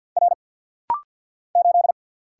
Morse code, heard as a keyed tone, about 50 wpm, 700 Hz. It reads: R 8
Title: Single Letter-Number - Rapid-Fire 50wpm